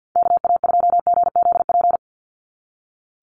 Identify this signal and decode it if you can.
Morse code, heard as a keyed tone, about 48 wpm, 700 Hz. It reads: KA2GZP
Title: USA Call Signs 50wpm